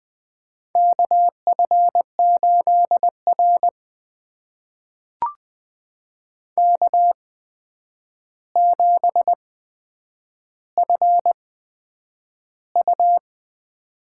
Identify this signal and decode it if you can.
Morse code, heard as a keyed tone, about 20 wpm, spaced as for 9 wpm, 700 Hz. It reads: KF8R K7FU
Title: USA Call Signs - 4 Characters - 8x Character Spacing - 20wpm